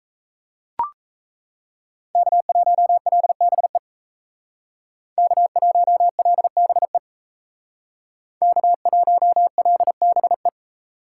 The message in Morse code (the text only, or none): K1LBE K1LBE K1LBE